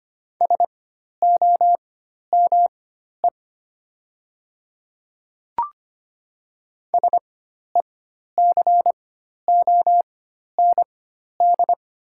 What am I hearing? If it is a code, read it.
Morse code, heard as a keyed tone, about 25 wpm, spaced as for 12 wpm, 700 Hz. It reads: SOME SECOND
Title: 4x Character Spacing - Top 500 Words - 25wpm